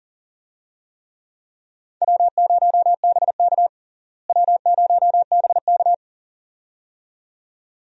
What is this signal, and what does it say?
Morse code, heard as a keyed tone, about 40 wpm, 700 Hz. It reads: W0BK W0BK
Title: USA Call Signs - 4 Characters - Repeated 2x - 40wpm